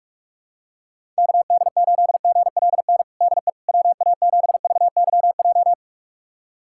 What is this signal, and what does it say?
Morse code, heard as a keyed tone, about 44 wpm, 700 Hz. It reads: KD8GLN DE WA7VYJ
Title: USA Callsign DE Callsign 45wpm